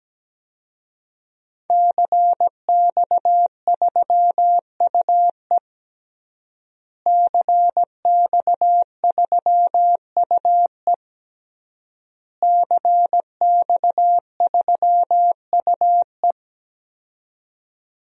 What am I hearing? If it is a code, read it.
Morse code, heard as a keyed tone, about 17 wpm, 700 Hz. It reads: CX3UE CX3UE CX3UE
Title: DXCC-100 Call Signs - 5 Characters - Repeated 3x - 17wpm